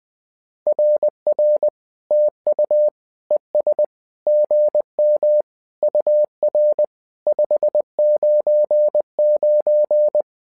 Morse code, heard as a keyed tone, about 20 wpm, 600 Hz. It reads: RR TU ES GM UR 599
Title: SOTA QSO Practice 20wpm